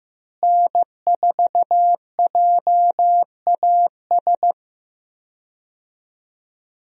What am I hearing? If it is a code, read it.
Morse code, heard as a keyed tone, about 15 wpm, 700 Hz. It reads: N4JAS
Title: USA Call Signs - 5 Characters - Repeated 3x - 15wpm